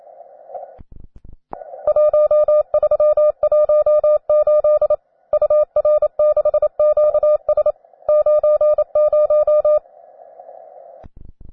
Morse code, heard as a keyed tone, about 27 wpm, 600 Hz. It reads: 1318 UR6QS 90
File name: cw advanced